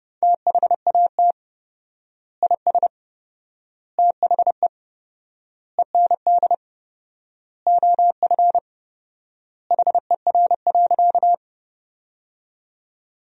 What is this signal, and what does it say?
Morse code, heard as a keyed tone, about 30 wpm, 700 Hz. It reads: THAT IS THE END OF HER.
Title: 4x Word-Spacing - Sentences from Top 200 Words - 30wpm